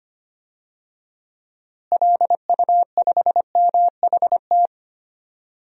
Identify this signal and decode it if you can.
Morse code, heard as a keyed tone, about 25 wpm, 700 Hz. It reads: LU5MHT
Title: DXCC-100 Call Signs - 6 Characters - Repeated 2x - 25wpm